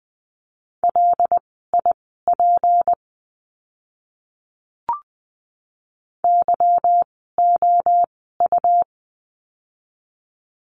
Morse code, heard as a keyed tone, about 20 wpm, spaced as for 14 wpm, 700 Hz. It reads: LIP YOU
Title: 153 Lesson 25f - Letter p - Words 20wpm